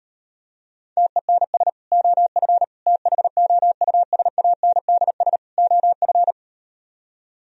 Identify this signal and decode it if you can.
Morse code, heard as a keyed tone, about 38 wpm, 700 Hz. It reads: TENS OF THOUSANDS OF